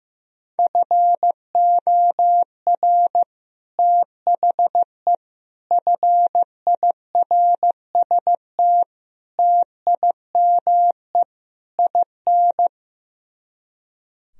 Morse code, heard as a keyed tone, about 16 wpm, 700 Hz. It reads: FOR THE FIRST TIME IN
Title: Speed-Racing - Sets of 5 Words 15wpm